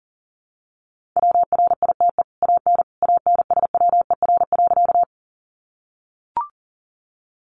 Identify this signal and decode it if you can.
Morse code, heard as a keyed tone, about 40 wpm, 700 Hz. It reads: WRITE AN ANSWER.
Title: Sentences from Top 300 Words 40wpm